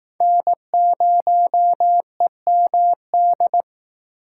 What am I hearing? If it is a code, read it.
Morse code, heard as a keyed tone, about 19 wpm, 700 Hz. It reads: N0EMD